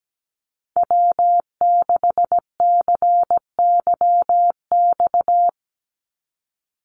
Morse code, heard as a keyed tone, about 17 wpm, 700 Hz. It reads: W6CYX
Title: Call Signs Repeated 3x - 17wpm